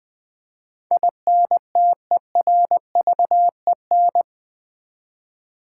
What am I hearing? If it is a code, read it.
Morse code, heard as a keyed tone, about 20 wpm, 700 Hz. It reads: INTERVEN
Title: Build A Word - 4 Letters At A Time - 12-Letter Words - 20wpm